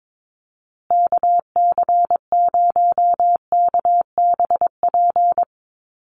Morse code, heard as a keyed tone, about 22 wpm, 700 Hz. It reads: KC0KBP
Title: USA Call Signs Repeated 3x 22wpm